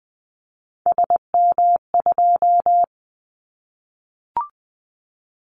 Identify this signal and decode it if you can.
Morse code, heard as a keyed tone, about 20 wpm, 700 Hz. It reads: SM2
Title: Callsign Prefixes for Top 25 DXCC - 20wpm